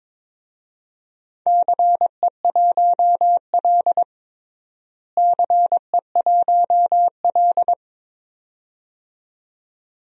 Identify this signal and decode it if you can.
Morse code, heard as a keyed tone, about 22 wpm, 700 Hz. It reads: CE1L CE1L
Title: DXCC-100 Call Signs - 4 Characters - Repeated 2x - 22wpm